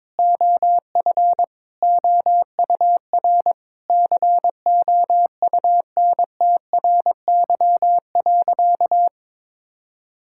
Morse code, heard as a keyed tone, about 21 wpm, 700 Hz. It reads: OF OUR COUNTRY.
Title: Sentences from Top 1000 Words - Review Part 6 - 22wpm